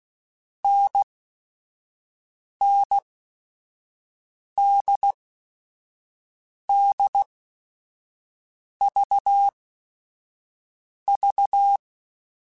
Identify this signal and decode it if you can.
Morse code, heard as a keyed tone, about 16 wpm, spaced as for 5 wpm, 800 Hz. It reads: NNDDVV
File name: Short Runs / Short Runs-mp3s / NDB NDB6 @5